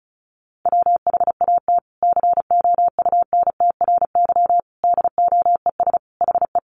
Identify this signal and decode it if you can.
Morse code, heard as a keyed tone, about 34 wpm, 700 Hz. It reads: WHAT COUNTRY DOES HE
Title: Sentences from Top 300 Words 35wpm